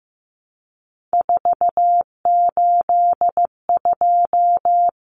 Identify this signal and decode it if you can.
Morse code, heard as a keyed tone, about 15 wpm, 700 Hz. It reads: 482
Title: Common Cut Numbers - 3-Digit 15wpm